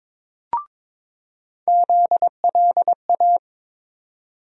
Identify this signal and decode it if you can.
Morse code, heard as a keyed tone, about 22 wpm, 700 Hz. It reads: ZLA
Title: USA Call Sign Postfixes - 3 Letters 22wpm